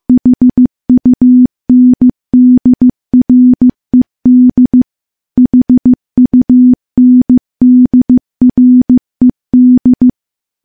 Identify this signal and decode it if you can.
Morse code, heard as a keyed tone, about 16 wpm, 250 Hz. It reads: HUNDRED HUNDRED
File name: morse / morse-training-15WPM